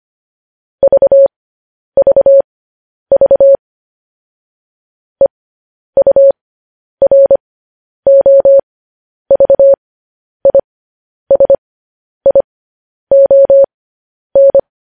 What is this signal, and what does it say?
Morse code, heard as a keyed tone, about 25 wpm, spaced as for 10 wpm, 550 Hz. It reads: VVV EUROVISION
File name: podcast / News headlines-1x-25x10